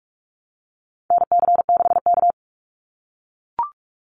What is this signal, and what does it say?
Morse code, heard as a keyed tone, about 44 wpm, 700 Hz. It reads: NC6K
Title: Call Signs - 45wpm